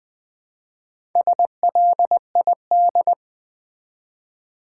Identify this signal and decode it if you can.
Morse code, heard as a keyed tone, about 20 wpm, 700 Hz. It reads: SLID
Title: Build A Word - 1 Letter At A Time - 5-Letter Words - 20wpm